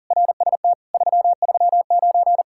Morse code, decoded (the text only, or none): RST 339